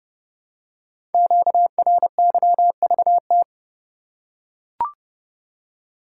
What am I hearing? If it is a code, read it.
Morse code, heard as a keyed tone, about 28 wpm, 700 Hz. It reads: QRYVT
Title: Code Groups - 5 Letter:Numbers - 30wpm